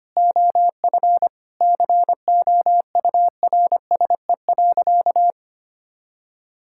Morse code, heard as a keyed tone, about 25 wpm, 700 Hz. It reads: OF COURSE.
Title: Sentences from Top 1000 Words - Review Part 5 - 25wpm